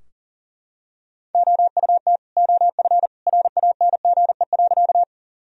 Morse code, heard as a keyed tone, about 38 wpm, 700 Hz. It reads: OUT OF RANGE.
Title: Sentences from Top 1000 Words - Review Part 4 - 40wpm